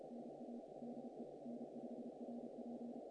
Morse code, heard as a keyed tone, about 33 wpm, 240 Hz. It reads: NA DE N5RP
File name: contests / naqp-cw-k5tr-op-k5pi- -1800